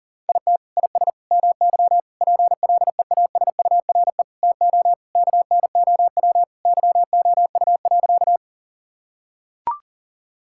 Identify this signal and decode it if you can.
Morse code, heard as a keyed tone, about 40 wpm, 700 Hz. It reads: IT IS MY PLEASURE TO KNOW YOU.
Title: Sentences from Top 1600 Words 40wpm